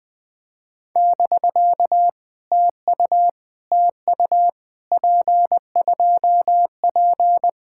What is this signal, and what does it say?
Morse code, heard as a keyed tone, about 18 wpm, 700 Hz. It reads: <BK> TU TU P2P
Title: POTA QSO Practice - 20wpm